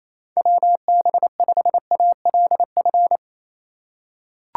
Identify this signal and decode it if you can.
Morse code, heard as a keyed tone, about 28 wpm, 700 Hz. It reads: WB5ALF
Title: USA Call Signs - 4x Character Spacing - 28wpm